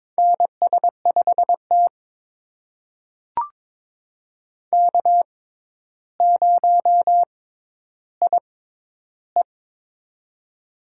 Morse code, heard as a keyed tone, about 22 wpm, spaced as for 11 wpm, 700 Hz. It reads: NS5T K0IE
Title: USA Call Signs - 4 Characters - 6x Character Spacing - 22wpm